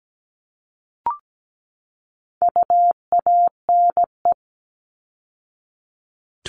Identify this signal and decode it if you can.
Morse code, heard as a keyed tone, about 17 wpm, 700 Hz. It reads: UANE
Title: All Cut Numbers - 4-Digit 17wpm